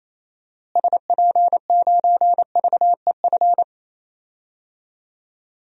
Morse code, heard as a keyed tone, about 28 wpm, 700 Hz. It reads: SP9VEF
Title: DXCC-100 Call Signs - 6 Characters - Repeated 2x - 28wpm